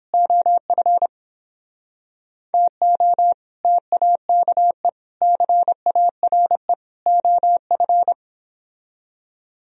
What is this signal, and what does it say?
Morse code, heard as a keyed tone, about 30 wpm, 700 Hz. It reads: OF TO TAKE CARE OF